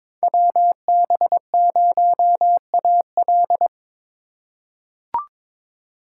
Morse code, heard as a keyed tone, about 22 wpm, 700 Hz. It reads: WB0AL
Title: USA Call Signs - 5 Characters - 2x Character Spacing - 22wpm